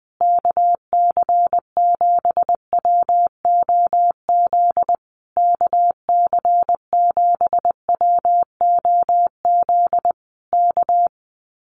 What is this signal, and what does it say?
Morse code, heard as a keyed tone, about 20 wpm, 700 Hz. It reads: KC7WOZ KC7WOZ K
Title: Common QSO Phrases Encore 20wpm